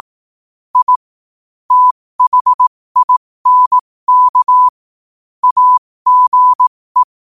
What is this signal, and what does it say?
Morse code, heard as a keyed tone, about 18 wpm, spaced as for 15 wpm, 1000 Hz. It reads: I THINK AGE
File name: qotdfiles / 15wpm